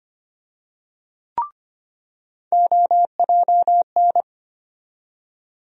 Morse code, heard as a keyed tone, about 25 wpm, 700 Hz. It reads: OJN
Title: USA Call Sign Postfixes - 3 Letters 25wpm